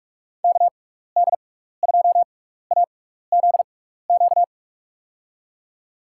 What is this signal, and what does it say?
Morse code, heard as a keyed tone, about 44 wpm, spaced as for 15 wpm, 700 Hz. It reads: KD2AZQ
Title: USA Call Signs - 6x Character Spacing - 45wpm